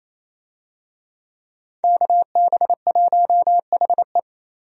Morse code, heard as a keyed tone, about 28 wpm, 700 Hz. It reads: KB1HE
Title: USA Call Signs - 5 Characters - Repeated 3x - 28wpm